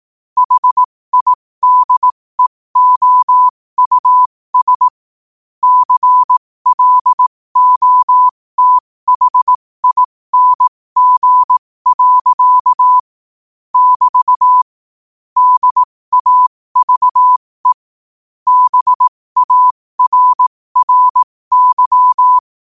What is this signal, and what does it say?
Morse code, heard as a keyed tone, about 18 wpm, spaced as for 15 wpm, 1000 Hz. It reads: HIDEOUS CLOTHING. <BT> DAVE BARRY
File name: qotdfiles / 15wpm